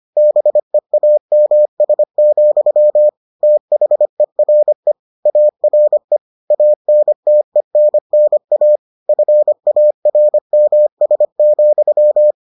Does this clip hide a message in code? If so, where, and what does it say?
Yes: Morse code from 0.2 to 12.3 s, heard as a keyed tone, about 25 wpm, 600 Hz. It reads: BEAMS, THERE ARE ANTENNA FARMS,